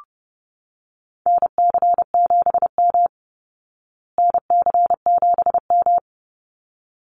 Morse code, heard as a keyed tone, about 28 wpm, 700 Hz. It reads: NC7M NC7M
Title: Call Signs Repeated 3x - 30wpm